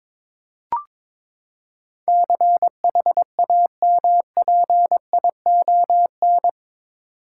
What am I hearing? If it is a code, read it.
Morse code, heard as a keyed tone, about 22 wpm, 700 Hz. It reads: CHAMPION
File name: practice / 22wpm / Eight Letter Words - 22wpm